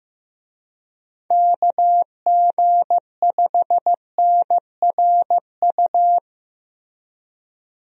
Morse code, heard as a keyed tone, about 14 wpm, 700 Hz. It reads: KG5NRU
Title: USA Call Signs - 6 Characters - Repeated 3x - 15wpm